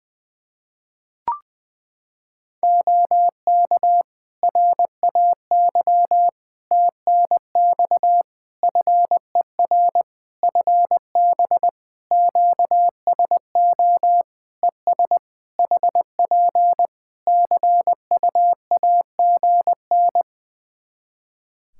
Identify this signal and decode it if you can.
Morse code, heard as a keyed tone, about 21 wpm, 700 Hz. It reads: OK RAY TNX FER FB QSO ES HP CUAGN
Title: Practice QSO - Exchange 4 - Ending - 20wpm